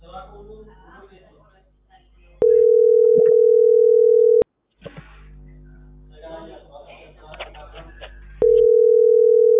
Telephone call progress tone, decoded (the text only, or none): ringback tone